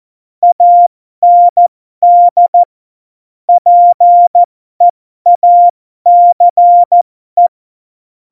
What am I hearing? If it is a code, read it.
Morse code, heard as a keyed tone, about 14 wpm, spaced as for 12 wpm, 700 Hz. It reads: AND PEACE
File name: Morsekurs / Witze